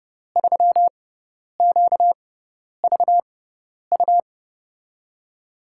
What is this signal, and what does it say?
Morse code, heard as a keyed tone, about 30 wpm, spaced as for 10 wpm, 700 Hz. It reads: 3QVU
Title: USA Call Signs - 5 Characters - 6x Character Spacing - 30wpm